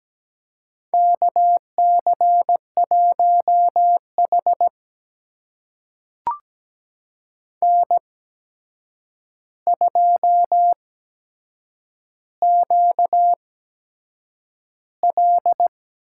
Morse code, heard as a keyed tone, about 17 wpm, spaced as for 7 wpm, 700 Hz. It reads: KC1H N2QL
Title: USA Call Signs - 4 Characters - 8x Character Spacing - 17wpm